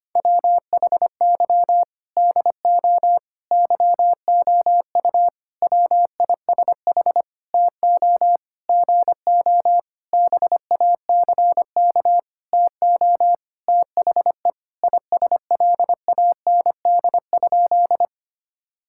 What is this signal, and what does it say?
Morse code, heard as a keyed tone, about 25 wpm, 700 Hz. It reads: WHY DO YOU WISH TO GO BACK TO THE ISLAND?
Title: Sentences from Top 1600 Words 25wpm